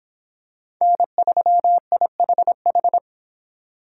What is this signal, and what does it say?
Morse code, heard as a keyed tone, about 24 wpm, 700 Hz. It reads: N3IHH